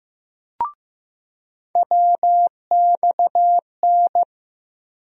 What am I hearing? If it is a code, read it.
Morse code, heard as a keyed tone, about 15 wpm, 700 Hz. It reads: WXN
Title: USA Call Sign Postfixes - 3 Letters 15wpm